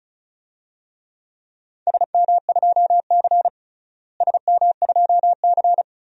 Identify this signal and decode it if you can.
Morse code, heard as a keyed tone, about 34 wpm, 700 Hz. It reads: SM2C SM2C
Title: DXCC-100 Call Signs - 4 Characters - Repeated 2x - 35wpm